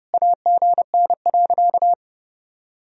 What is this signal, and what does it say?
Morse code, heard as a keyed tone, about 32 wpm, 700 Hz. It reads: AGN.
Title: Sentences from Top 500 Words and Abbreviations 30wpm